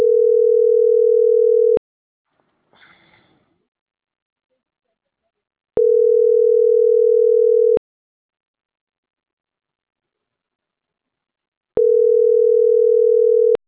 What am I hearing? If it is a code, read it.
Ringback tone (North America), 440 + 480 Hz.
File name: CENTRAL MAERSK / out andres.rdzmsk